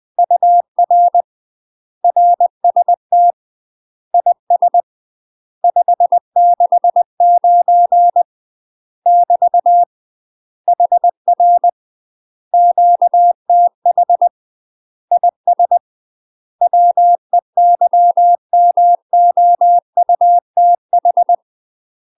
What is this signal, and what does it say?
Morse code, heard as a keyed tone, about 20 wpm, 700 Hz. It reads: UR RST IS 569 <BT> HR QTH IS WEYMOUTH
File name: UK MASTER QSOs TRAINING / 20WPM / UK MASTER QSOs 0071